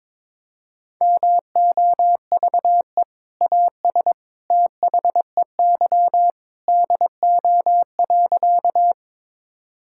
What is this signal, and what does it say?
Morse code, heard as a keyed tone, about 22 wpm, 700 Hz. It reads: MOVE AS THEY DO.
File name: practice / 22wpm / Sentences with 2-Word Phrases - 22wpm